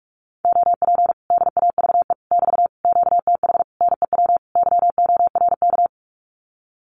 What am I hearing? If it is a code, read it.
Morse code, heard as a keyed tone, about 44 wpm, 700 Hz. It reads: OP DAVE <BT> QTH NEW YORK
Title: Common QSO Phrases Encore 45wpm